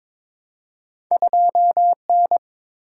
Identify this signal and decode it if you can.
Morse code, heard as a keyed tone, about 22 wpm, 700 Hz. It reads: 2N
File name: practice / 22wpm / Two Letter-Number Combinations - 22wpm